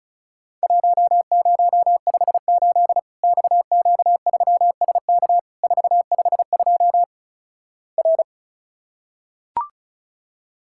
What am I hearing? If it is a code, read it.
Morse code, heard as a keyed tone, about 36 wpm, 700 Hz. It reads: 1058 XQ3SK 452 R
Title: Worked All Europe DX Contest - QTC Exchange - 35wpm